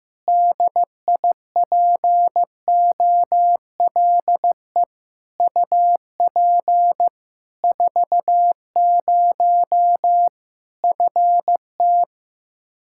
Morse code, heard as a keyed tone, about 15 wpm, 700 Hz. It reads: DIPOLE UP 40 FT
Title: Practice QSO - Exchange 2 - Info - 15wpm